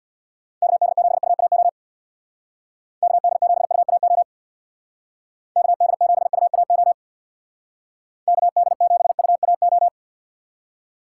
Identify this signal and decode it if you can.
Morse code, heard as a keyed tone, about 63 wpm, 700 Hz. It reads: KD7UAO KD7UAO KD7UAO KD7UAO